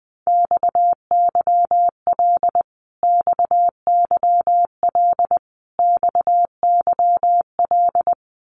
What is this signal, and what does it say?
Morse code, heard as a keyed tone, about 20 wpm, 700 Hz. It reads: XYL XYL XYL
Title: Abbreviations Repeated 3x - 20wpm